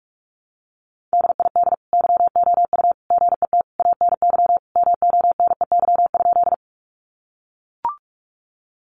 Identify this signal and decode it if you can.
Morse code, heard as a keyed tone, about 44 wpm, 700 Hz. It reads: DID YOU GET ANY MONEY?
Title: Sentences from Top 400 Words 45wpm